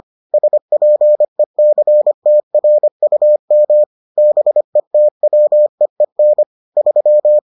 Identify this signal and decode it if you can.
Morse code, heard as a keyed tone, about 25 wpm, 600 Hz. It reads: SPECTRUM BETWEEN 3